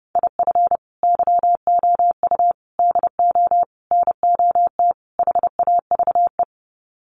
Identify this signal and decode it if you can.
Morse code, heard as a keyed tone, about 30 wpm, 700 Hz. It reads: IF YOU DO NOT HAVE
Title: Sets of 5 Words 30wpm